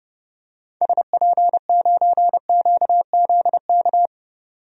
Morse code, heard as a keyed tone, about 29 wpm, 700 Hz. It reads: SP9QZK
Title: DXCC-100 Call Signs - 6 Characters - 30wpm